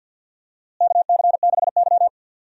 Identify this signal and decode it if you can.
Morse code, heard as a keyed tone, about 52 wpm, 700 Hz. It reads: KC6Y
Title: USA Call Signs - 4 Characters - Repeated 3x - 50wpm